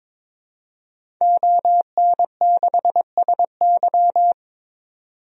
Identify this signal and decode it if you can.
Morse code, heard as a keyed tone, about 22 wpm, 700 Hz. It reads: ON6SY
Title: DXCC-100 Call Signs - 5 Characters - 22wpm